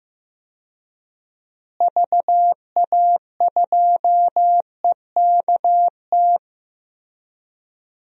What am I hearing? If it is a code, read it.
Morse code, heard as a keyed tone, about 15 wpm, 700 Hz. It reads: VA2EKT
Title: DXCC-100 Call Signs - 6 Characters - Repeated 2x - 15wpm